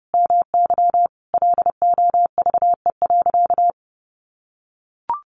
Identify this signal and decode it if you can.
Morse code, heard as a keyed tone, about 30 wpm, 700 Hz. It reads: MY LOVE.
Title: Sentences from Top 1100 Words 30wpm